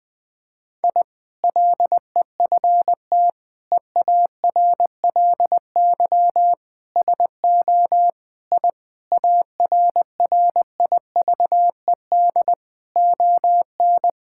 Morse code, heard as a keyed tone, about 20 wpm, 700 Hz. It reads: I LEFT EARLY SO I ARRIVED ON
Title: Sentences with 2-Word Phrases - 20wpm